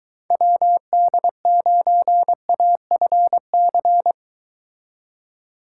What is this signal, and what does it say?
Morse code, heard as a keyed tone, about 23 wpm, 700 Hz. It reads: WD9AFC